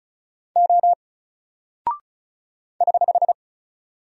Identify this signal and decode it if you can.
Morse code, heard as a keyed tone, about 36 wpm, 700 Hz. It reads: O <HH>
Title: Single Character (Letter:Number:Punctuation:Prosign) 35wpm